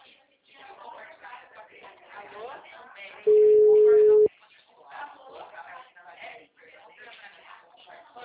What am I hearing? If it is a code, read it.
Ringback tone (Europe, 425 Hz), with speech.